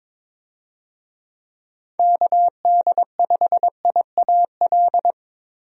Morse code, heard as a keyed tone, about 22 wpm, 700 Hz. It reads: KD5IAL
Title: USA Call Signs - 6 Characters - Repeated 2x - 22wpm